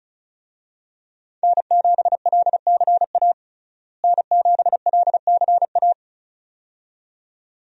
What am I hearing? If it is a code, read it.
Morse code, heard as a keyed tone, about 34 wpm, 700 Hz. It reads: N7LCA N7LCA
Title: USA Call Signs - 5 Characters - Repeated 2x - 35wpm